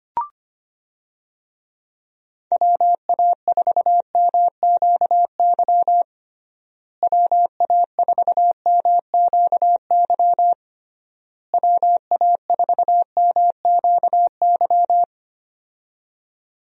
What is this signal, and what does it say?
Morse code, heard as a keyed tone, about 25 wpm, 700 Hz. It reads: WA4MQY WA4MQY WA4MQY
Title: USA Call Signs - 6 Characters - Repeated 3x - 25wpm